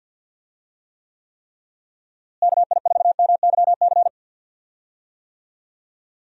Morse code, heard as a keyed tone, about 49 wpm, 700 Hz. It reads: KI4MYC
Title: USA Call Signs - 6 Characters - Repeated 3x - 50wpm